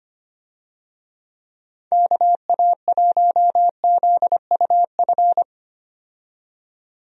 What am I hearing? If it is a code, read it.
Morse code, heard as a keyed tone, about 25 wpm, 700 Hz. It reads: KA1ZUF
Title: USA Call Signs - 6 Characters - 25wpm